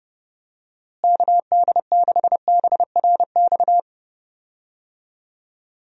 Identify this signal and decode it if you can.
Morse code, heard as a keyed tone, about 30 wpm, 700 Hz. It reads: KD6BRX
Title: USA Call Signs - 6 Characters - Repeated 3x - 30wpm